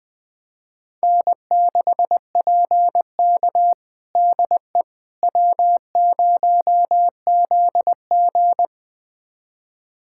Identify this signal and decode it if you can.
Morse code, heard as a keyed tone, about 20 wpm, 700 Hz. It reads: N6PK DE W0ZG
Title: USA Callsign DE Callsign - 4 Characters - 20wpm